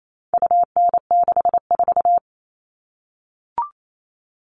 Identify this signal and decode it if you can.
Morse code, heard as a keyed tone, about 28 wpm, 700 Hz. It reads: UN64
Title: All Cut Numbers - 4-Digit 28wpm